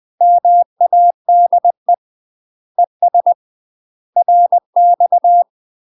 Morse code, heard as a keyed tone, about 20 wpm, 700 Hz. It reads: MADE ES RX
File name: UK MASTER QSOs TRAINING / 20WPM / UK MASTER QSOs 0048